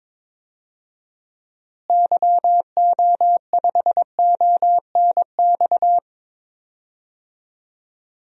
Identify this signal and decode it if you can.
Morse code, heard as a keyed tone, about 21 wpm, 700 Hz. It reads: YO5ONX
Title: DXCC-100 Call Signs - 6 Characters - Repeated 2x - 22wpm